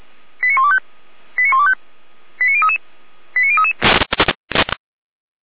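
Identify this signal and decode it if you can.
Five-tone selective call (ZVEI), heard as quick runs of five tones, tones of about 70 ms. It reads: 89125 89125 89033 89033
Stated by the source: ZVEI five-tone selective call received on an SDR.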